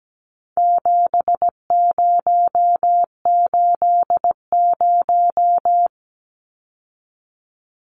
Morse code, heard as a keyed tone, about 17 wpm, 700 Hz. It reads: 7080
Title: Numbers - 4-Digit 17wpm